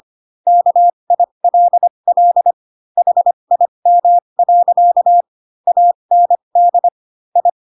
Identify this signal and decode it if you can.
Morse code, heard as a keyed tone, about 25 wpm, spaced as for 22 wpm, 700 Hz. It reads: KILL HIM. AND I
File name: Pinocchio / Pinocchio ch7 18 25 ik1hge